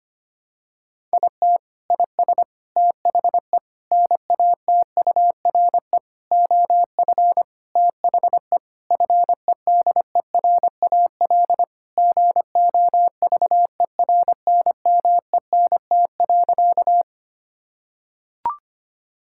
Morse code, heard as a keyed tone, about 25 wpm, 700 Hz. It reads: IT IS THE NATURE OF THE FEDERAL GOVERNMENT.
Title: Sentences from Top 1000 Words - Review Part 5 - 25wpm